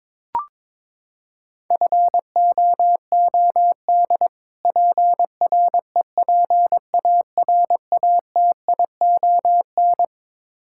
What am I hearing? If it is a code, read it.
Morse code, heard as a keyed tone, about 22 wpm, 700 Hz. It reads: FOOD PREPARATION
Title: K1USN SST QSO Practice 22wpm